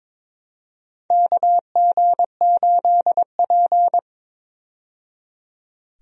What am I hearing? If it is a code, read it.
Morse code, heard as a keyed tone, about 22 wpm, 700 Hz. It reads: KG8P
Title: USA Call Signs - 4 Characters - 22wpm